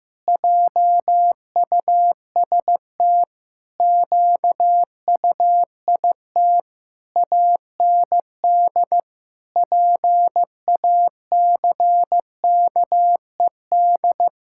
Morse code, heard as a keyed tone, about 16 wpm, 700 Hz. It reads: JUST QUIT AND PACKED
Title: Pangram Challenge v2 - 15wpm